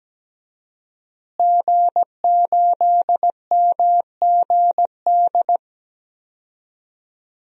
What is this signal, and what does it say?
Morse code, heard as a keyed tone, about 17 wpm, 700 Hz. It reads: G8MGD
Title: DXCC-100 Call Signs - 5 Characters - Repeated 2x - 17wpm